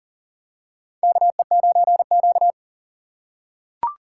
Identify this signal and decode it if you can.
Morse code, heard as a keyed tone, about 40 wpm, 700 Hz. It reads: KE9Q
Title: USA Call Signs - 4 Characters - 3x Character Spacing - 40wpm